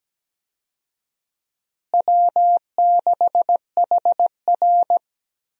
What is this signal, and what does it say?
Morse code, heard as a keyed tone, about 17 wpm, 700 Hz. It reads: W6HR
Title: USA Call Signs - 4 Characters - Repeated 3x - 17wpm